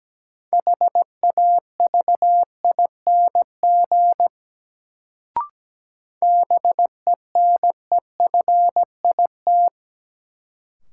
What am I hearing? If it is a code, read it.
Morse code, heard as a keyed tone, about 17 wpm, 700 Hz. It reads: HAVING BENEFIT
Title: Sets of 5 Words Encore - New Word Drill - 17wpm